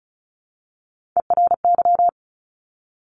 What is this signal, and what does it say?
Morse code, heard as a keyed tone, about 36 wpm, 700 Hz. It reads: ERY
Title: Top Three Letter Combinations in Words 35wpm